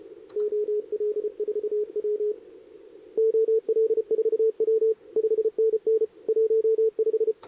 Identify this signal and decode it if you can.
Morse code, heard as a keyed tone, about 30 wpm, 440 Hz, with radio noise. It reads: OL4W OL4W 5NN 15
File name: CQWWCW2023 / SP